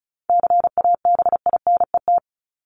Morse code, heard as a keyed tone, about 34 wpm, 700 Hz. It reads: CABINET
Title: Top 1751-2000 Words 35wpm